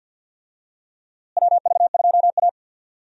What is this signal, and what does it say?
Morse code, heard as a keyed tone, about 53 wpm, 700 Hz. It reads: WV2A
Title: USA Call Signs - 4 Characters - Repeated 2x - 50wpm